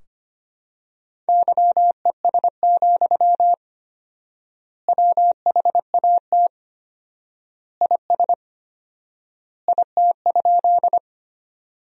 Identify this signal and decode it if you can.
Morse code, heard as a keyed tone, about 25 wpm, 700 Hz. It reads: YES, WHAT IS IT?
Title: 4x Word-Spacing - Sentences from Top 500 Words - 25wpm